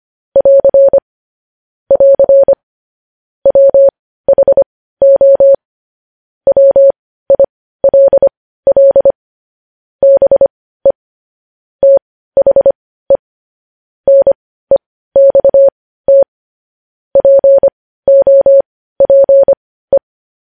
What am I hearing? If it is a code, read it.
Morse code, heard as a keyed tone, about 25 wpm, spaced as for 15 wpm, 550 Hz. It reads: <AR> <AR> WHO WILL BE THE NEXT POPE
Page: News Headlines in Morse Code at 15 WPM